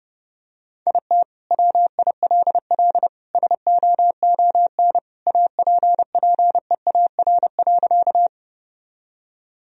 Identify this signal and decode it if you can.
Morse code, heard as a keyed tone, about 30 wpm, 700 Hz. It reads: IT WILL SOON APPEAR.
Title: Sentences from Top 1000 Words - Review Part 2 - 30wpm